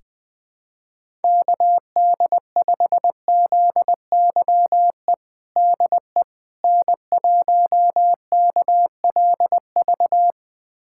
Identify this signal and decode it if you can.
Morse code, heard as a keyed tone, about 20 wpm, 700 Hz. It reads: KD5ZYE DE N1KLV
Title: USA Callsign DE Callsign 20wpm